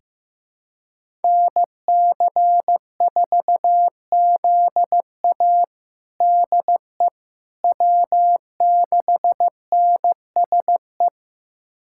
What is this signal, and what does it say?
Morse code, heard as a keyed tone, about 15 wpm, 700 Hz. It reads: NC4ZA DE W6NSE
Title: USA Callsign DE Callsign - 5 Characters - 15wpm